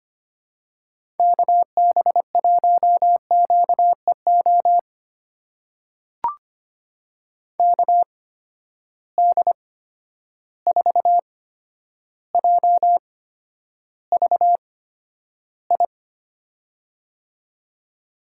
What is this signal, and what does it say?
Morse code, heard as a keyed tone, about 25 wpm, spaced as for 11 wpm, 700 Hz. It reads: KB1QEO KD4JVI
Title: USA Call Signs - 6 Characters - 8x Character Spacing - 25wpm